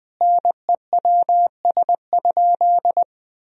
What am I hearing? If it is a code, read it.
Morse code, heard as a keyed tone, about 20 wpm, 700 Hz. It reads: NEWS?
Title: Sentences from Top 1000 Words - Review Part 4 - 20wpm